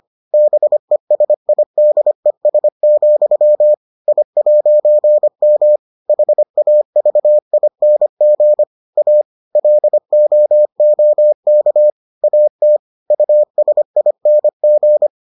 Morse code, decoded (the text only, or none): BESIDES, I'M HAVING A LOOK AT USING